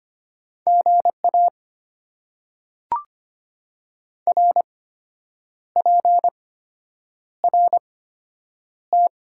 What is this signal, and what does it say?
Morse code, heard as a keyed tone, about 25 wpm, spaced as for 8 wpm, 690 Hz. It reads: GA RPRT
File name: Top 90 QSO Elements - 8x Character Spacing - 25wpm